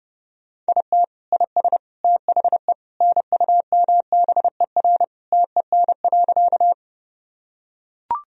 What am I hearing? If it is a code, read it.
Morse code, heard as a keyed tone, about 30 wpm, 700 Hz. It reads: IT IS THE NUMBER TEN.
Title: Sentences from Top 1000 Words - Review Part 2 - 30wpm